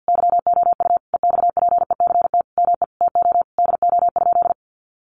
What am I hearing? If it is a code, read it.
Morse code, heard as a keyed tone, about 50 wpm, 700 Hz. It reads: YOU EXPECT ME TO DO?
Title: Sentences from Top 1100 Words 50wpm